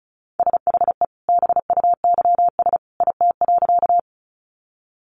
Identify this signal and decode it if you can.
Morse code, heard as a keyed tone, about 34 wpm, 700 Hz. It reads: SHE BUYS IT.
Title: Additional Easy Phrases 35wpm